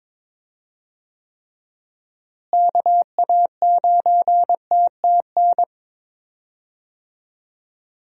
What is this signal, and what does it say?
Morse code, heard as a keyed tone, about 22 wpm, 700 Hz. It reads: KA9TTN
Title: USA Call Signs - 6 Characters - Repeated 2x - 22wpm